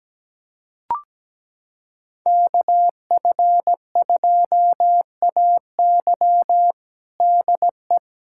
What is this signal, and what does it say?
Morse code, heard as a keyed tone, about 17 wpm, 700 Hz. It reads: KF2AY DE
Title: USA Callsign DE Callsign - 5 Characters - 17wpm